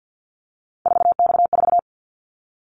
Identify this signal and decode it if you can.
Morse code, heard as a keyed tone, about 50 wpm, 700 Hz. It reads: 4X4
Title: Callsign Prefixes for Top 50 DXCC - 50wpm